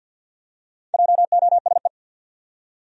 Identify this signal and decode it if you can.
Morse code, heard as a keyed tone, about 52 wpm, 700 Hz. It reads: JOSE
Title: USA Common Names - Max 5 Letters - 50wpm